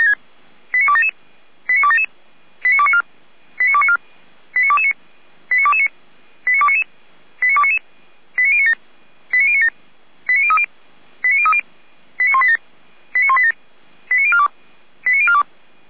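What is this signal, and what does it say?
Five-tone selective call (ZVEI), heard as quick runs of five tones, tones of about 70 ms. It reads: 89277 89277 89294 89294 89229 89229 89299 89299 89097 89097 89030 89030 89187 89187 89042 89042
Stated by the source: Five-tone ZVEI selective call received on SDR.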